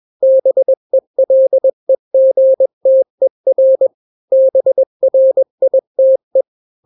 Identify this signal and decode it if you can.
Morse code, heard as a keyed tone, about 21 wpm, 530 Hz. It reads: BELEGTER BRITE
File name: daily / 20wpm